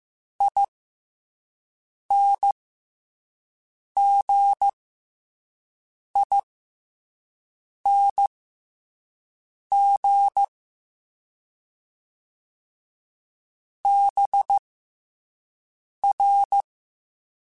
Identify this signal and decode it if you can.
Morse code, heard as a keyed tone, about 15 wpm, spaced as for 5 wpm, 800 Hz. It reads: INGING BR